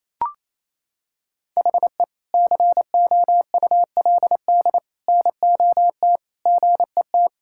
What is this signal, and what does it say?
Morse code, heard as a keyed tone, about 28 wpm, 700 Hz. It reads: HE COULD NOT GET
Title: Sentences from Top 1000 Words - Review Part 5 - 28wpm